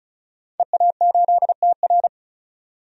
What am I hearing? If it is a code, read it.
Morse code, heard as a keyed tone, about 34 wpm, 700 Hz. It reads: EA8TR
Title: DXCC-100 Call Signs - 5 Characters - 35wpm